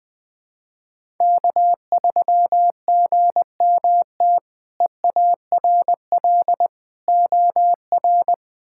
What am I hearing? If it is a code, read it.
Morse code, heard as a keyed tone, about 20 wpm, 700 Hz. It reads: K3GMT EARL OR
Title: K1USN SST QSO Practice - 20wpm